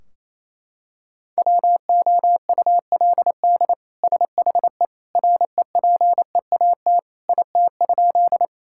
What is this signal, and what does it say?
Morse code, heard as a keyed tone, about 28 wpm, 700 Hz. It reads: WOULD SHE REPEAT IT?
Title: Sentences from Top 1600 Words 28wpm